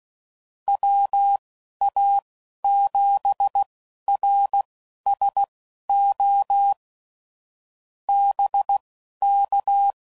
Morse code, heard as a keyed tone, about 16 wpm, spaced as for 11 wpm, 800 Hz. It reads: WA7RSO BK
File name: CW Code Groups @15WPM / QSOs / WD0FBT Spaced